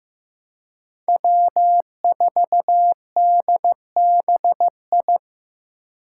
Morse code, heard as a keyed tone, about 15 wpm, 700 Hz. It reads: W4DBI
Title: Five Challenging Letter-Number Combinations 15wpm